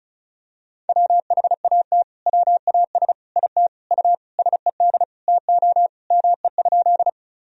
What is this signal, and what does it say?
Morse code, heard as a keyed tone, about 34 wpm, 690 Hz. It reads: WHAT WAS IT U SED TO ME?
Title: Sentences from Top 200 Words and Abbreviations 35wpm